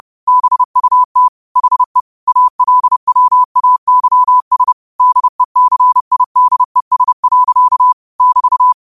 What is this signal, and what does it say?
Morse code, heard as a keyed tone, about 30 wpm, 1000 Hz. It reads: BUT HE ALWAYS DECIDES. <BT>
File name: qotdfiles / 30wpm